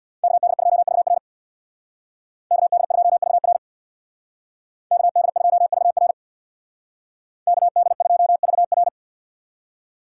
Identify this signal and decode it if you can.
Morse code, heard as a keyed tone, about 64 wpm, 690 Hz. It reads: KD2VR KD2VR KD2VR KD2VR